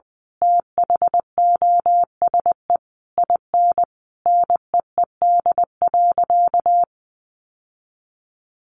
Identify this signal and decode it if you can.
Morse code, heard as a keyed tone, about 20 wpm, 700 Hz. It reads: THOSE IN NEED.
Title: Sentences from Top 1200 Words 20wpm